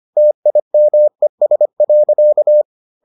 Morse code, heard as a keyed tone, about 25 wpm, 600 Hz. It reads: TIMES.